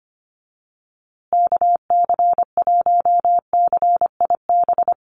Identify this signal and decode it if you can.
Morse code, heard as a keyed tone, about 25 wpm, 700 Hz. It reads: KC1CIB
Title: USA Call Signs Repeated 3x 25wpm